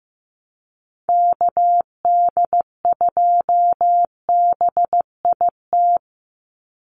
Morse code, heard as a keyed tone, about 15 wpm, 700 Hz. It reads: KD2BIT
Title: USA Call Signs 15wpm